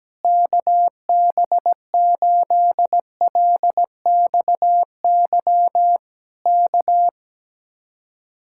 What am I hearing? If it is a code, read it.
Morse code, heard as a keyed tone, about 17 wpm, 700 Hz. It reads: KB8LXY K
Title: Speed-Racing Common QSO Phrases - 17wpm